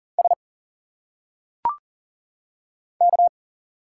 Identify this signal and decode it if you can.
Morse code, heard as a keyed tone, about 40 wpm, 700 Hz. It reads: S K
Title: Single Letters - Phonetic Pronunciation 40wpm